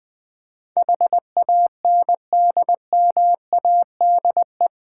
Morse code, heard as a keyed tone, about 21 wpm, 700 Hz. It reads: HANDMADE
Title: Difficult Words with Hints - 20wpm